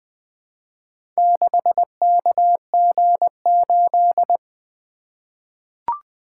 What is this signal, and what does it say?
Morse code, heard as a keyed tone, about 20 wpm, 700 Hz. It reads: 6KG8
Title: Four Letter-Number Combinations - 20wpm